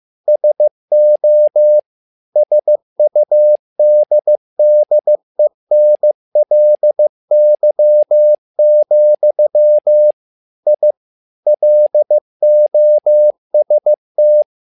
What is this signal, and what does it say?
Morse code, heard as a keyed tone, about 15 wpm, 600 Hz. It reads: SO SUDDENLY, I LOST
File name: learning center / tower trouble / tower trouble 15wpm